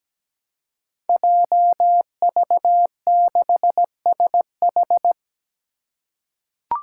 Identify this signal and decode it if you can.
Morse code, heard as a keyed tone, about 17 wpm, 700 Hz. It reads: JV6SH
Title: Five Challenging Letter-Number Combinations 17wpm